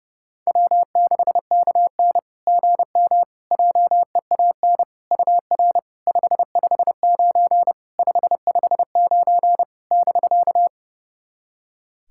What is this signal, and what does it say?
Morse code, heard as a keyed tone, about 30 wpm, 700 Hz. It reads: W6KN GM JEAN UR 559 559 <BK>
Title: SOTA QSO Practice 30wpm